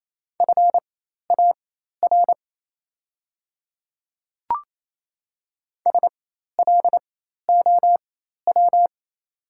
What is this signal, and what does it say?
Morse code, heard as a keyed tone, about 27 wpm, spaced as for 13 wpm, 690 Hz. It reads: FAR SLOW
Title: 4x Character Spacing - Top 500 Words - 28wpm